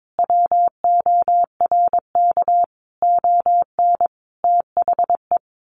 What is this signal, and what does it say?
Morse code, heard as a keyed tone, about 22 wpm, 700 Hz. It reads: WORK ON THE
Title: Sentences from Top 1200 Words 22wpm